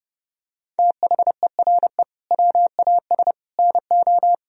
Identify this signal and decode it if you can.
Morse code, heard as a keyed tone, about 30 wpm, 700 Hz. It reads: THERE WAS NO
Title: Sentences from Top 1000 Words - Review Part 3 - 30wpm